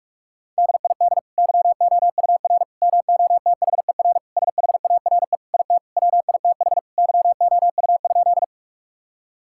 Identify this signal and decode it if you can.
Morse code, heard as a keyed tone, about 46 wpm, 700 Hz. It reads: DID YOUR MOTHER SHARE IT WITH YOU?
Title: Sentences from Top 1000 Words - Review Part 6 - 45wpm